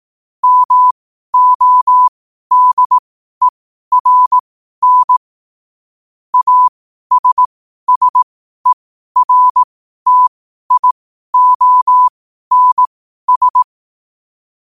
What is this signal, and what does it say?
Morse code, heard as a keyed tone, about 18 wpm, spaced as for 12 wpm, 1000 Hz. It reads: MODERN ASSERTIONS
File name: qotdfiles / 12wpm